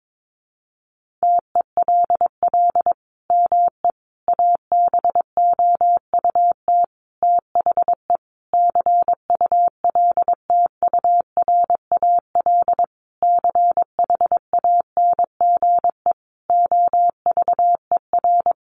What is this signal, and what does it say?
Morse code, heard as a keyed tone, about 22 wpm, 700 Hz. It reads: TELL ME ABOUT THE CULTURAL CHANGE OVER
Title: Sentences from Top 1500 Words 22wpm